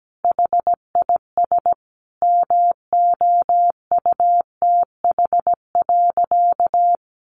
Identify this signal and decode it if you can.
Morse code, heard as a keyed tone, about 17 wpm, 700 Hz. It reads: HIS MOUTH.
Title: Sentences from Top 1500 Words 17wpm